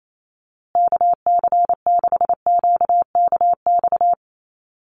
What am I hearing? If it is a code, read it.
Morse code, heard as a keyed tone, about 28 wpm, 690 Hz. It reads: KC6QKX
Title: USA Call Signs Repeated 3x 28wpm